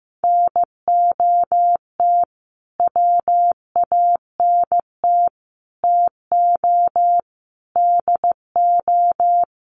Morse code, heard as a keyed tone, about 15 wpm, 700 Hz. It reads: NOT WANT TO DO
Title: Sets of 5 Words Encore 15wpm